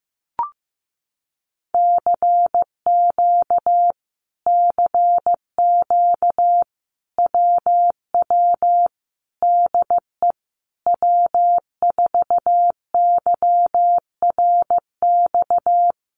Morse code, heard as a keyed tone, about 14 wpm, 700 Hz. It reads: CQ CQ WW DE W4YRX
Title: Common QSO Phrases Encore 15wpm